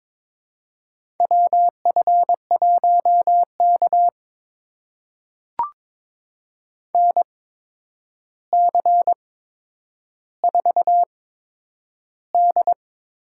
Morse code, heard as a keyed tone, about 22 wpm, spaced as for 9 wpm, 700 Hz. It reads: WF1K NC4D
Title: USA Call Signs - 4 Characters - 8x Character Spacing - 22wpm